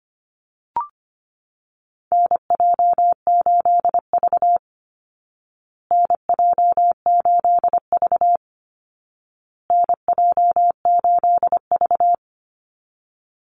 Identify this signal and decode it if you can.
Morse code, heard as a keyed tone, about 25 wpm, 700 Hz. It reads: NJ8V NJ8V NJ8V
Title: Call Signs Repeated 2x - 25wpm